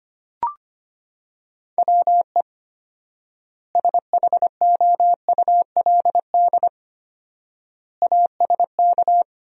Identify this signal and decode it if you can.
Morse code, heard as a keyed tone, about 25 wpm, 700 Hz. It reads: WE SHOULD ASK
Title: 4x Word-Spacing - Sentences from Top 300 Words - 25wpm